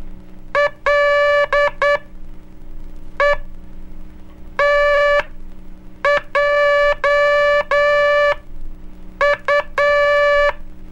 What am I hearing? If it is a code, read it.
Morse code, heard as a keyed tone, about 9 wpm, spaced as for 5 wpm, 580 Hz. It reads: LETJU
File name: cursos / Curso CW mp3 / CW Aula - 04